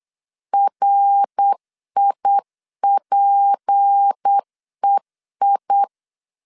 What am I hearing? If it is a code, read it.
Morse code, heard as a keyed tone, about 8 wpm, 800 Hz. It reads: RIPEI
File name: LEZIONI / 6 PKX0 / 6 P 8wpm 3025